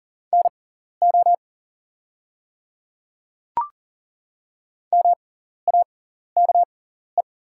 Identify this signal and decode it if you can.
Morse code, heard as a keyed tone, about 40 wpm, spaced as for 14 wpm, 690 Hz. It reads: NO MAKE
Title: 6x Character Spacing - Top 100 Words - 40wpm